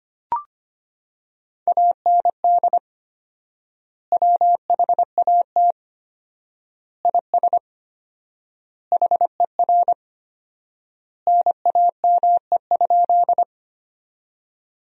Morse code, heard as a keyed tone, about 25 wpm, 700 Hz. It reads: AND WHAT IS HER NAME?
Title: 4x Word-Spacing - Sentences from Top 200 Words - 25wpm